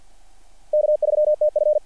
Morse code, heard as a keyed tone, about 50 wpm, 600 Hz. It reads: K3TF
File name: archived / CRSnet